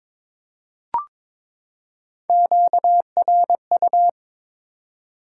Morse code, heard as a keyed tone, about 21 wpm, 700 Hz. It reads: QRU
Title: Top 90 QSO Elements 22wpm